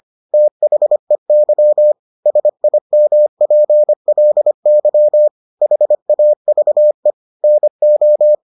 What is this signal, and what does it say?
Morse code, heard as a keyed tone, about 25 wpm, 600 Hz. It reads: THEY SIMPLY HAVE NO